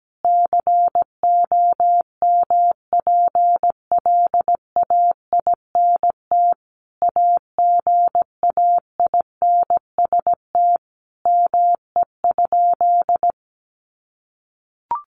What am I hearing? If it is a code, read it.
Morse code, heard as a keyed tone, about 17 wpm, 700 Hz. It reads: COMPLAINT AGAINST ME?
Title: Sentences from Top 1400 Words 17wpm